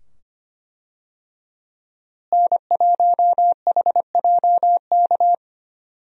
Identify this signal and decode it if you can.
Morse code, heard as a keyed tone, about 25 wpm, 690 Hz. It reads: N1HJK
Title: USA Call Signs - 5 Characters - Repeated 2x - 25wpm